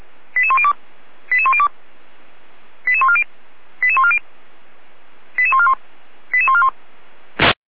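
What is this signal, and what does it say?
Five-tone selective call (ZVEI), heard as quick runs of five tones, tones of about 70 ms. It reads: 88102 88102 88140 88140 88141 88141
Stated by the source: ZVEI five-tone selective call 88501 received by SDR.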